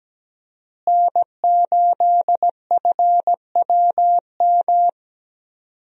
Morse code, heard as a keyed tone, about 17 wpm, 700 Hz. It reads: N8FWM
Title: USA Call Signs - 5 Characters - Repeated 2x - 17wpm